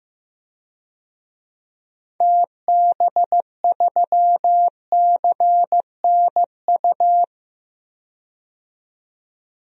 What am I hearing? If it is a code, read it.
Morse code, heard as a keyed tone, about 15 wpm, 700 Hz. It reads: TB3CNU
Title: DXCC-100 Call Signs - 6 Characters - 15wpm